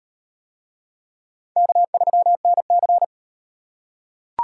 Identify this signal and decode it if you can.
Morse code, heard as a keyed tone, about 38 wpm, 700 Hz. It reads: K3NC